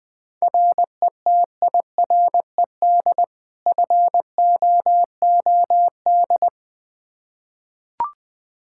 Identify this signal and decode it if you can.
Morse code, heard as a keyed tone, about 19 wpm, 700 Hz. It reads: RETIRED FOOD
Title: K1USN SST QSO Practice 20wpm